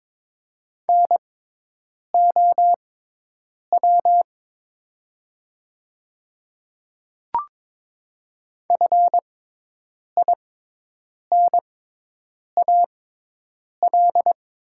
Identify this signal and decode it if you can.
Morse code, heard as a keyed tone, about 23 wpm, spaced as for 8 wpm, 700 Hz. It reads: NOW FINAL
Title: 6x Character Spacing - Top 500 Words - 22wpm